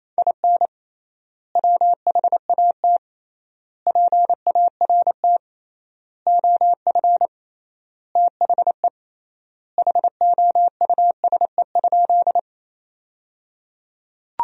Morse code, heard as a keyed tone, about 28 wpm, 700 Hz. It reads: IN WHAT PART OF THE HOUSE?
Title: 3x Word-Spacing - Sentences from Top 200 Words - 28wpm